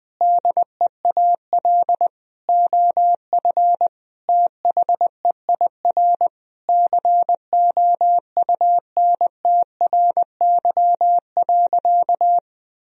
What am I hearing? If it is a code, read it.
Morse code, heard as a keyed tone, about 21 wpm, 700 Hz. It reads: DEAL OF THEIR COUNTRY.
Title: Sentences from Top 1000 Words - Review Part 1 - 20wpm